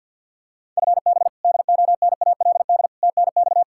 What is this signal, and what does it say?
Morse code, heard as a keyed tone, about 48 wpm, 700 Hz. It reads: FB DONALD TNX
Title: Practice QSO - Exchange 3 - More Info - 50wpm